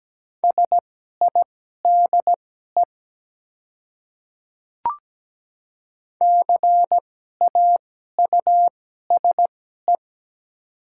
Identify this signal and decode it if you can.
Morse code, heard as a keyed tone, about 17 wpm, spaced as for 12 wpm, 700 Hz. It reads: SIDE CAUSE
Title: 2x Character Spacing - Top 200 Words - 17wpm